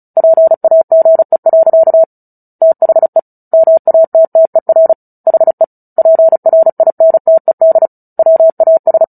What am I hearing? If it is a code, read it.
Morse code, heard as a keyed tone, about 34 wpm, 650 Hz. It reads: PAGE. THE MATTER HE PRINTED WAS
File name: Book Two / 09-Chapter 9 - 35 WPM